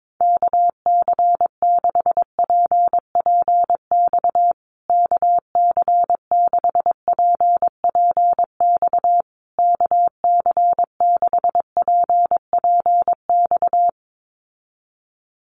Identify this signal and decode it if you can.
Morse code, heard as a keyed tone, about 22 wpm, 700 Hz. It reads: KC6PPX KC6PPX KC6PPX
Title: USA Call Signs Repeated 3x 22wpm